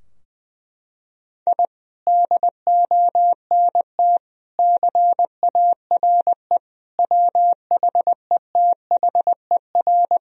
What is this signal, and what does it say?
Morse code, heard as a keyed tone, about 21 wpm, 700 Hz. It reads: I DONT CARE WHETHER
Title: Sentences from Top 1000 Words - Review Part 3 - 20wpm